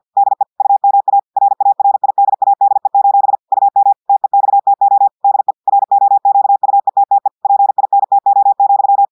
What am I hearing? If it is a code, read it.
Morse code, heard as a keyed tone, about 50 wpm, 800 Hz. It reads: DE UMA RARIDADE: UM TEXTO DE ROQUETTE PINTO,
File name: audios / portugais / portugais texte en clair 1 vitesse 50 mots minutes